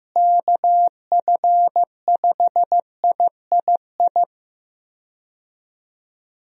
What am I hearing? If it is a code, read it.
Morse code, heard as a keyed tone, about 15 wpm, 700 Hz. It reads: KF5III
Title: USA Call Signs - 6 Characters - Repeated 3x - 15wpm